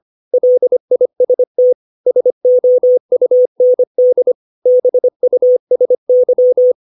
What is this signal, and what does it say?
Morse code, heard as a keyed tone, about 25 wpm, 500 Hz. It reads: LIST SOUND BUSY
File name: 632 words / 632 Words 25WPM